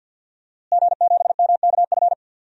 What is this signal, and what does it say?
Morse code, heard as a keyed tone, about 52 wpm, 700 Hz. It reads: G7MKF
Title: DXCC-100 Call Signs - 5 Characters - Repeated 2x - 50wpm